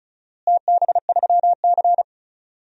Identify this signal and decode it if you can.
Morse code, heard as a keyed tone, about 36 wpm, 700 Hz. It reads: TB3C
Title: DXCC-100 Call Signs - 4 Characters - Repeated 2x - 35wpm